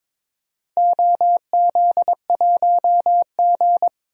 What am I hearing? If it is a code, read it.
Morse code, heard as a keyed tone, about 21 wpm, 700 Hz. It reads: OZ1G
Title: DXCC-100 Call Signs - 4 Characters - 22wpm